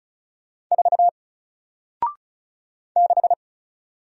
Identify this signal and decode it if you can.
Morse code, heard as a keyed tone, about 36 wpm, 700 Hz. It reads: 4 6
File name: Single-Letter-Number-35wpm